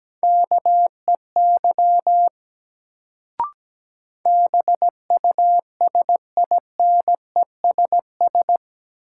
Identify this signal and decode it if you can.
Morse code, heard as a keyed tone, about 17 wpm, 700 Hz. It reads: KEY BUSINESS
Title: Sentences from Top 600 Words - New Word Drill - 17wpm